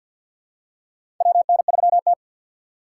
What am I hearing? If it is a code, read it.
Morse code, heard as a keyed tone, about 51 wpm, 700 Hz. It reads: WN3T
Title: USA Call Signs - 3x Character Spacing - 50wpm